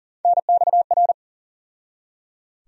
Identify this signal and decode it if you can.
Morse code, heard as a keyed tone, about 38 wpm, 700 Hz. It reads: NXR
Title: USA Call Sign Postfixes - 3 Letters 40wpm